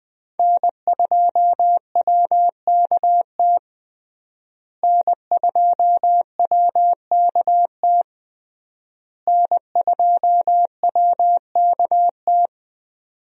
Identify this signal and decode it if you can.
Morse code, heard as a keyed tone, about 21 wpm, 700 Hz. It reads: N2WKT N2WKT N2WKT
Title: USA Call Signs - 5 Characters - Repeated 3x - 20wpm